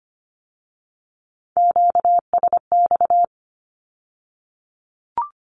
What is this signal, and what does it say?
Morse code, heard as a keyed tone, about 25 wpm, 700 Hz. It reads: QSX
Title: Q-Codes with Spoken Meaning - 25wpm